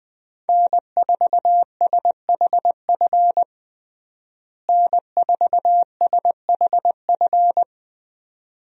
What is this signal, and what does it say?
Morse code, heard as a keyed tone, about 20 wpm, 700 Hz. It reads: N4SHF N4SHF
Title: USA Call Signs - 5 Characters - Repeated 2x - 20wpm